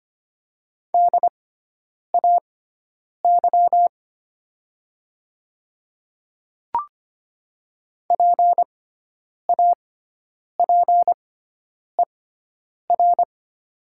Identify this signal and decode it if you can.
Morse code, heard as a keyed tone, about 25 wpm, spaced as for 9 wpm, 700 Hz. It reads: DAY PAPER
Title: 6x Character Spacing - Top 300 Words - 25wpm